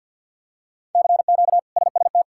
Morse code, heard as a keyed tone, about 52 wpm, 700 Hz. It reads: CQ SST
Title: K1USN SST QSO Practice - 50wpm